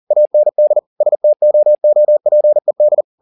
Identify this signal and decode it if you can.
Morse code, heard as a keyed tone, about 40 wpm, 600 Hz. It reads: AND STOOPED
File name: the old man and the sea 2 40wpm